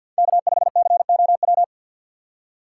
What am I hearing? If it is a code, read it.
Morse code, heard as a keyed tone, about 52 wpm, 700 Hz. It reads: K5COW
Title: USA Call Signs - 5 Characters - 6x Character Spacing - 50wpm